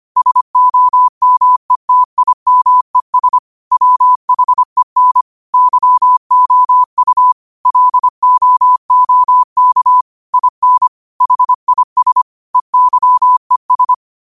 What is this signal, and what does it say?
Morse code, heard as a keyed tone, about 25 wpm, 1000 Hz. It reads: SOMETIMES WHEN YOU LOOK IN HIS EYES
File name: qotdfiles / 25wpm